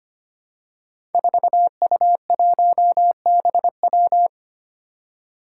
Morse code, heard as a keyed tone, about 25 wpm, 700 Hz. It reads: 4U1BW
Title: Five Challenging Letter-Number Combinations 25wpm